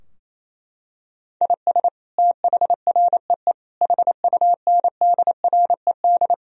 Morse code, heard as a keyed tone, about 28 wpm, 700 Hz. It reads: IS THREE HUNDRED
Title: Sentences from Top 1000 Words - Review Part 2 - 28wpm